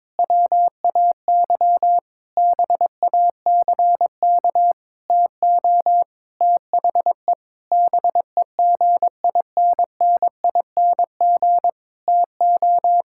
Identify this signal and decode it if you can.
Morse code, heard as a keyed tone, about 22 wpm, 700 Hz. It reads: WAY BACK TO THE BEGINNING TO
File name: practice / 22wpm / Sentences with 8-Word Phrases - 22wpm